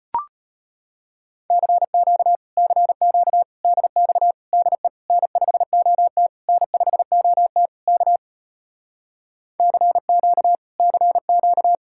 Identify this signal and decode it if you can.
Morse code, heard as a keyed tone, about 33 wpm, 700 Hz. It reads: CQ CQ DX DE N5OT N5OT K CQ CQ